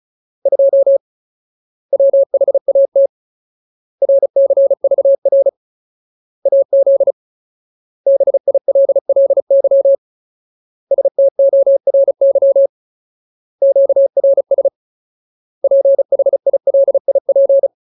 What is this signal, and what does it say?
Morse code, heard as a keyed tone, about 34 wpm, 550 Hz. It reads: 2 WHAT RCVR AZ BILLY STORY QRS PHILIP
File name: Ultimate Word List / 35WPM Ultimate